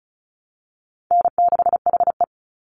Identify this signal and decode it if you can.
Morse code, heard as a keyed tone, about 34 wpm, 700 Hz. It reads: N6HE
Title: Easy Call Signs Repeated 2x - 35wpm